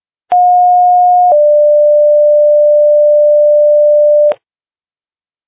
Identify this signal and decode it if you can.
Two-tone page, heard as a one-second tone, then a three-second tone, about 720 then 590 Hz. It is heard from 0.3 to 4.3 s.